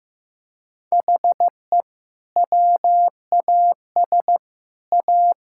Morse code, heard as a keyed tone, about 15 wpm, 700 Hz. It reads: HE WAS A
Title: Sentences from Top 1000 Words - Review Part 6 - 15wpm